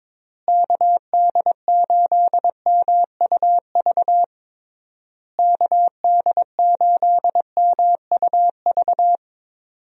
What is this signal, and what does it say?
Morse code, heard as a keyed tone, about 22 wpm, 700 Hz. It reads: KD8MUV KD8MUV
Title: USA Call Signs - 6 Characters - Repeated 3x - 22wpm